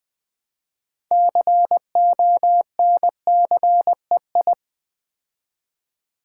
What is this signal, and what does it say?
Morse code, heard as a keyed tone, about 20 wpm, 700 Hz. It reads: CONCEI
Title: Build A Word - 2 Letters At A Time - 8-Letter Words - 20wpm